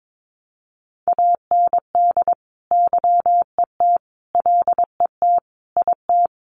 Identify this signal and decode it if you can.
Morse code, heard as a keyed tone, about 22 wpm, 700 Hz. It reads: AND YET LET IT
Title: Sentences from Top 1200 Words 22wpm